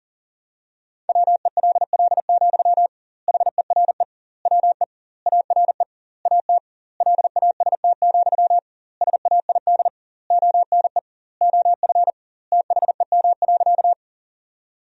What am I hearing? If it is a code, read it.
Morse code, heard as a keyed tone, about 40 wpm, 700 Hz. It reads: WELL, HERE WE ARE AT LAST, SAID ONE OF THEM.
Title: 2x Word-Spacing - Sentences from Top 300 Words - 40wpm